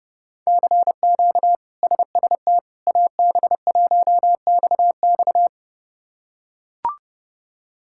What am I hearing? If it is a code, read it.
Morse code, heard as a keyed tone, about 30 wpm, 700 Hz. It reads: CQ SST AB1XX
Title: K1USN SST QSO Practice - 30wpm